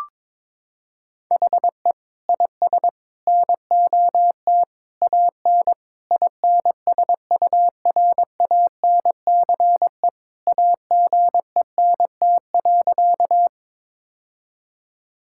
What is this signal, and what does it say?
Morse code, heard as a keyed tone, about 21 wpm, 700 Hz. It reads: HE IS NOT AN INSURANCE AGENT.
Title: Sentences from Top 1600 Words 22wpm